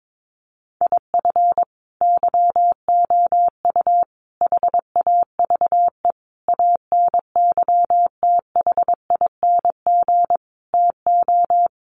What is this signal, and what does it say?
Morse code, heard as a keyed tone, about 22 wpm, 700 Hz. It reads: IF YOU HAVE ANYTHING TO
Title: Sentences from Top 1000 Words 22wpm